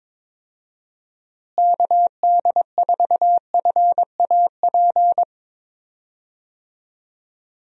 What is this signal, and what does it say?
Morse code, heard as a keyed tone, about 23 wpm, 700 Hz. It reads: KD4FAP
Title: USA Call Signs - 6 Characters - Repeated 2x - 22wpm